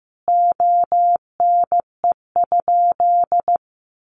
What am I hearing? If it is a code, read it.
Morse code, heard as a keyed tone, about 14 wpm, 700 Hz. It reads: ONE?
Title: Sentences from Top 900 Words 15wpm